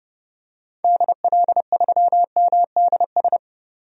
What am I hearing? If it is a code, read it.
Morse code, heard as a keyed tone, about 30 wpm, 700 Hz. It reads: DL3MDS
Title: DXCC-100 Call Signs - 6 Characters - 30wpm